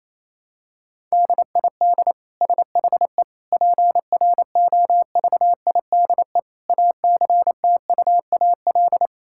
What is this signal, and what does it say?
Morse code, heard as a keyed tone, about 27 wpm, 700 Hz. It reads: DID SHE PROVIDE ACTUAL
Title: Sentences from Top 1000 Words - Review Part 5 - 28wpm